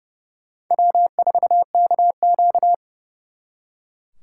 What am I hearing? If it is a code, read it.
Morse code, heard as a keyed tone, about 37 wpm, 700 Hz. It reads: W4KQ
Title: Speed Racing - USA 1x2 Call Signs 30wpm